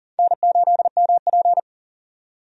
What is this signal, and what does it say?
Morse code, heard as a keyed tone, about 39 wpm, 700 Hz. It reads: N8MP
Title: USA Call Signs - 4 Characters - 40wpm